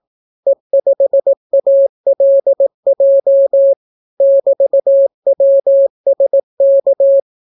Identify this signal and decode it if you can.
Morse code, heard as a keyed tone, about 18 wpm, 550 Hz. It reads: E5ALJ <BT>WSK